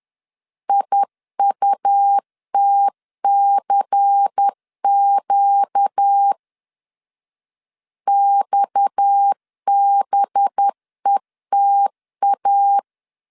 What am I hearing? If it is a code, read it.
Morse code, heard as a keyed tone, about 10 wpm, 800 Hz. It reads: IUTCQ XBETA
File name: CW / DET / 1 9 1500C S B 10wpm 4439